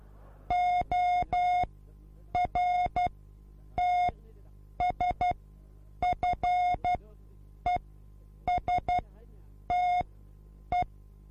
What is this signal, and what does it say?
Morse code, heard as a keyed tone, about 12 wpm, spaced as for 8 wpm, 720 Hz. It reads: ORTSFESTE